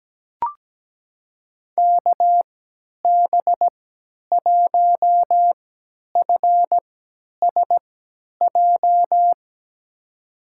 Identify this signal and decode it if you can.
Morse code, heard as a keyed tone, about 17 wpm, spaced as for 10 wpm, 700 Hz. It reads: KB1FSJ
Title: USA Call Signs - 6 Characters - 3x Character Spacing - 17wpm